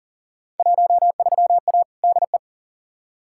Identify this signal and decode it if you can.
Morse code, heard as a keyed tone, about 40 wpm, 700 Hz. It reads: 13A DE
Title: Field Day Exchanges Encore - 40wpm